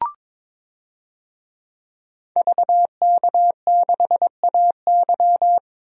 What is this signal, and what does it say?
Morse code, heard as a keyed tone, about 22 wpm, 700 Hz. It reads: VK6AY
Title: DXCC-100 Call Signs - 5 Characters - 22wpm